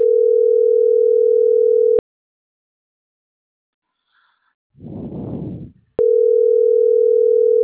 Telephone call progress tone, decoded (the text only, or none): ringback tone